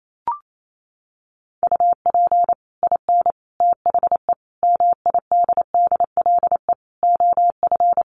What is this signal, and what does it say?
Morse code, heard as a keyed tone, about 28 wpm, 700 Hz. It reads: UP IN THE MIDDLE OF
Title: Sets of 5 Words Encore 28wpm